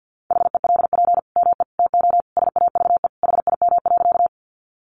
Morse code, heard as a keyed tone, about 50 wpm, 700 Hz. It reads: HELP ME TO SAVE HIM.
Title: Sentences from Top 1100 Words 50wpm